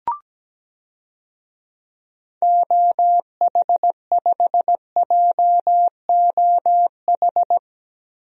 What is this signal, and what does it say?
Morse code, heard as a keyed tone, about 17 wpm, 700 Hz. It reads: OH5JOH
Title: DXCC-100 Call Signs - 6 Characters - Repeated 2x - 17wpm